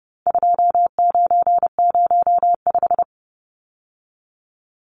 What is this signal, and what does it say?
Morse code, heard as a keyed tone, about 30 wpm, 700 Hz. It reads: 2905
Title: Numbers - 4-Digit 30wpm